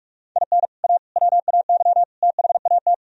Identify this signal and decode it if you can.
Morse code, heard as a keyed tone, about 49 wpm, 700 Hz. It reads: IN A WAY THAT